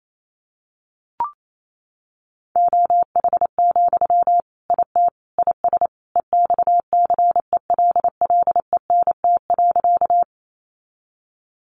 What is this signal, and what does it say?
Morse code, heard as a keyed tone, about 28 wpm, 700 Hz. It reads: OH, IT IS EXCELLENT.
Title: Sentences from Top 1000 Words 28wpm